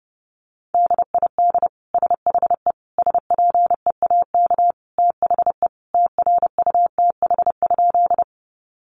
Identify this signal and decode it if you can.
Morse code, heard as a keyed tone, about 30 wpm, 700 Hz. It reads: DID SHE SPEAK THE TRUTH?
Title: Sentences from Top 1100 Words 30wpm